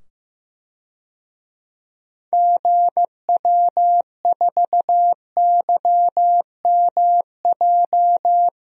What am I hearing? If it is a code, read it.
Morse code, heard as a keyed tone, about 15 wpm, 700 Hz. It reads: GW4YMJ
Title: DXCC-100 Call Signs - 6 Characters - 15wpm